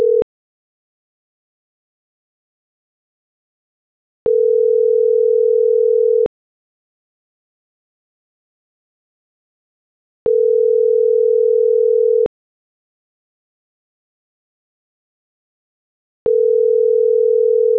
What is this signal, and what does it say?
Ringback tone (North America), 440 + 480 Hz.
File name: CENTRAL TERRESTRE / out karis - .2190